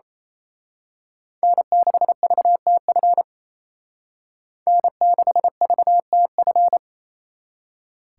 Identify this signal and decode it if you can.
Morse code, heard as a keyed tone, about 30 wpm, 700 Hz. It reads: N6VTF N6VTF
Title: Speed Racing - USA 1x3 Call Signs 28wpm